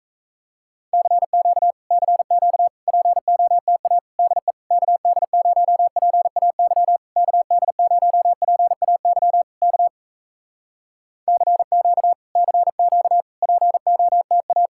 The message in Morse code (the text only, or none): CQ CQ POTA DE KD0PAY KD0PAY K CQ CQ POTA